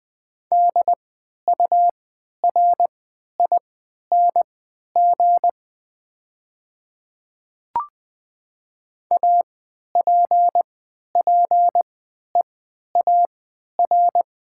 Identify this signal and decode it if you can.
Morse code, heard as a keyed tone, about 20 wpm, spaced as for 11 wpm, 700 Hz. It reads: DURING APPEAR
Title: 3x Character Spacing - Top 500 Words - 20wpm